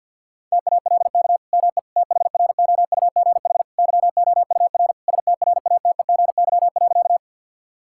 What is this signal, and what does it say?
Morse code, heard as a keyed tone, about 51 wpm, 700 Hz. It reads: TALK ME THROUGH YOUR STRATEGY.
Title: Sentences from Top 1000 Words - Review Part 5 - 50wpm